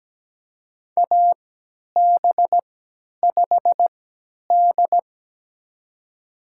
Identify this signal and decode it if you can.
Morse code, heard as a keyed tone, about 17 wpm, spaced as for 10 wpm, 700 Hz. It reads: AB5D
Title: USA Call Signs - 4 Characters - 3x Character Spacing - 17wpm